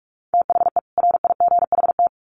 Morse code, heard as a keyed tone, about 44 wpm, 700 Hz. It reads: THE RIGHT
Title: Sets of 2 Words 45wpm